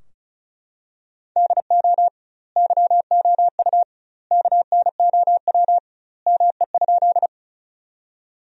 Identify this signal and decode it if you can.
Morse code, heard as a keyed tone, about 36 wpm, 700 Hz. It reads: DO YOU KNOW ME?
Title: 2x Word-Spacing - Sentences from Top 200 Words - 35wpm